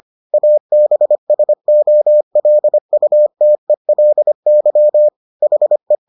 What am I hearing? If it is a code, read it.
Morse code, heard as a keyed tone, about 25 wpm, 600 Hz. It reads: ABSOLUTELY HE